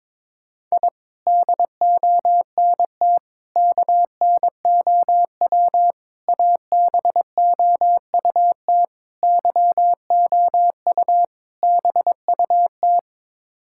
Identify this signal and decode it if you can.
Morse code, heard as a keyed tone, about 22 wpm, 700 Hz. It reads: I DONT KNOW ABOUT YOU BUT
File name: practice / 22wpm / Sets of 6 Words - 22wpm